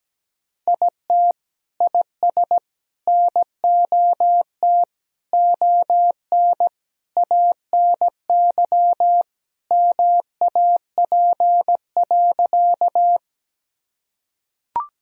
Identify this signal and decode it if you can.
Morse code, heard as a keyed tone, about 17 wpm, 700 Hz. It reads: IT IS NOT ON ANY MAP.
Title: Sentences from Top 1000 Words - Review Part 2 - 17wpm